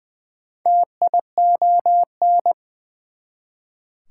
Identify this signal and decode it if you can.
Morse code, heard as a keyed tone, about 21 wpm, 700 Hz. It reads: TION
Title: Prefixes and Suffixes 20wpm